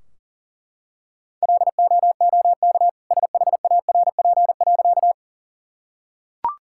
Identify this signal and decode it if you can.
Morse code, heard as a keyed tone, about 40 wpm, 700 Hz. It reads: LOOK SHARP.
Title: Sentences from Top 1600 Words 40wpm